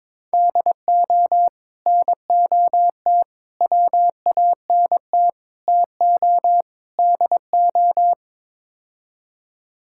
Morse code, heard as a keyed tone, about 22 wpm, 700 Hz. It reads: DO NOT WANT TO DO
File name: practice / 22wpm / Build A Phrase - 5-Word Phrases - 22wpm